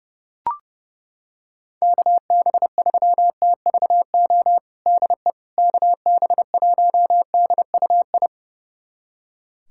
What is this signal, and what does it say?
Morse code, heard as a keyed tone, about 32 wpm, 700 Hz. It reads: KB3TVO DE KB1DUI
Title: USA Callsign DE Callsign - 6 Characters - 30wpm